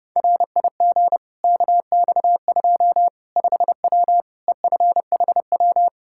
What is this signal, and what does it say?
Morse code, heard as a keyed tone, about 30 wpm, 700 Hz. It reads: RIG KX2 5W EFHW
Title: Practice QSO - Exchange 2 - Info - 30wpm